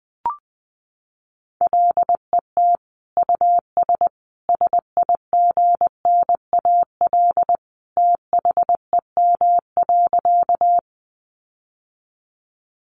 Morse code, heard as a keyed tone, about 19 wpm, 700 Hz. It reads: LET US SIGNAL THEM.
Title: Sentences from Top 1400 Words 20wpm